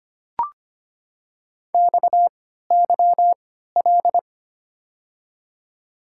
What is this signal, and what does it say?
Morse code, heard as a keyed tone, about 25 wpm, 700 Hz. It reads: X Y L
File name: Top 90 QSO Elements - 3x Character Spacing - 25wpm